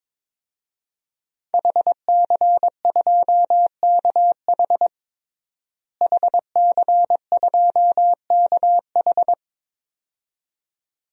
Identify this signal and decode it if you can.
Morse code, heard as a keyed tone, about 22 wpm, 700 Hz. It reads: HC2KH HC2KH
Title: DXCC-100 Call Signs - 5 Characters - Repeated 2x - 22wpm